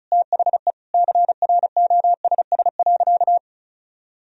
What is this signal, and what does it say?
Morse code, heard as a keyed tone, about 36 wpm, 700 Hz. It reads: THE CROSS.
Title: Sentences from Top 1000 Words - Review Part 1 - 35wpm